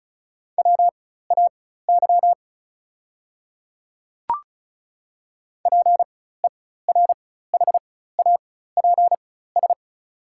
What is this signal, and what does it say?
Morse code, heard as a keyed tone, about 34 wpm, spaced as for 16 wpm, 690 Hz. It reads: WAY PERHAPS
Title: 4x Character Spacing - Top 500 Words - 35wpm